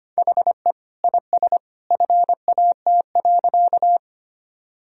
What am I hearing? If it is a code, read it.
Morse code, heard as a keyed tone, about 25 wpm, 700 Hz. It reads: HE IS FAT.
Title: Sentences from Top 1000 Words - Review Part 4 - 25wpm